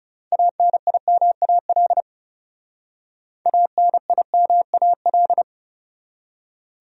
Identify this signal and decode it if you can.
Morse code, heard as a keyed tone, about 39 wpm, 700 Hz. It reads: ANIMAL ANIMAL